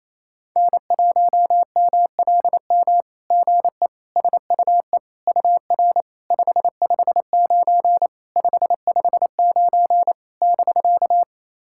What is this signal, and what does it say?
Morse code, heard as a keyed tone, about 28 wpm, 660 Hz. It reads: N1MLM GE SUE UR 559 559 <BK>
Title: SOTA QSO Practice 28wpm